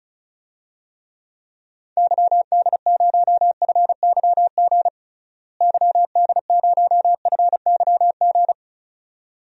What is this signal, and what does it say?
Morse code, heard as a keyed tone, about 34 wpm, 700 Hz. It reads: YD0FYG YD0FYG
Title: DXCC-100 Call Signs - 6 Characters - Repeated 2x - 35wpm